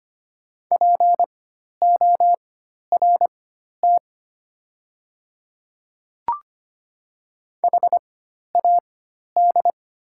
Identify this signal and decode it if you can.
Morse code, heard as a keyed tone, about 25 wpm, spaced as for 12 wpm, 690 Hz. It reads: PORT HAD
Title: 4x Character Spacing - Top 300 Words - 25wpm